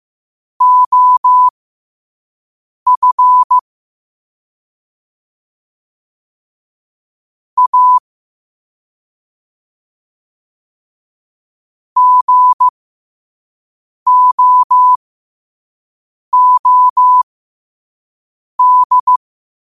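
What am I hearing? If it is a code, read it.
Morse code, heard as a keyed tone, about 15 wpm, spaced as for 5 wpm, 1000 Hz. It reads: OF A GOOD
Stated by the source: 5 wpm Morse.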